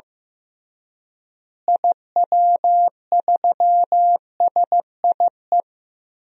Morse code, heard as a keyed tone, about 15 wpm, 700 Hz. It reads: IW3SIE
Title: DXCC-100 Call Signs - 6 Characters - Repeated 3x - 15wpm